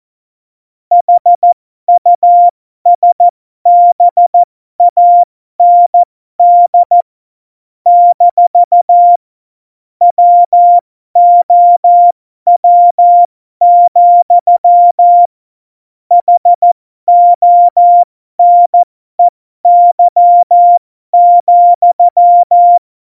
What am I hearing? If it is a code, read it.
Morse code, heard as a keyed tone, about 14 wpm, spaced as for 12 wpm, 700 Hz. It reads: HUSBAND - WOW, HONEY,
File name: Morsekurs / Witze